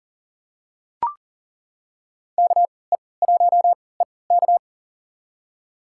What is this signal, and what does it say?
Morse code, heard as a keyed tone, about 39 wpm, spaced as for 22 wpm, 700 Hz. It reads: KE1EK
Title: USA Call Signs - 5 Characters - 3x Character Spacing - 40wpm